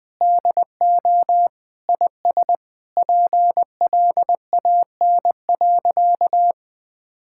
Morse code, heard as a keyed tone, about 21 wpm, 700 Hz. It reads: DO IS PLAN.
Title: Sentences with 8-Word Phrases - Spoken Beginning - 20wpm